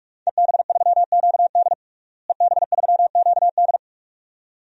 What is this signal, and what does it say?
Morse code, heard as a keyed tone, about 46 wpm, 700 Hz. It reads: EB3QD EB3QD
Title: DXCC-100 Call Signs - 5 Characters - Repeated 2x - 45wpm